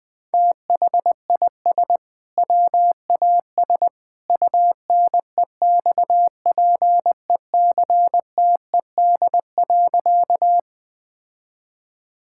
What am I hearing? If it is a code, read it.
Morse code, heard as a keyed tone, about 20 wpm, 700 Hz. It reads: THIS WAS UNEXPECTED.
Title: Sentences with 2-Word Phrases - 20wpm